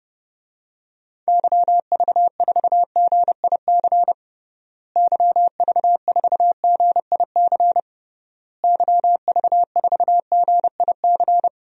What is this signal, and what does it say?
Morse code, heard as a keyed tone, about 30 wpm, 700 Hz. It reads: YV4GIC YV4GIC YV4GIC
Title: DXCC-100 Call Signs - 6 Characters - Repeated 3x - 30wpm